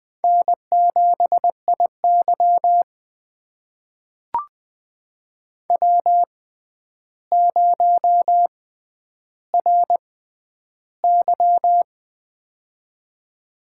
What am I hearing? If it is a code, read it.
Morse code, heard as a keyed tone, about 20 wpm, spaced as for 10 wpm, 700 Hz. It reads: N7IY W0RY
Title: USA Call Signs - 4 Characters - 6x Character Spacing - 20wpm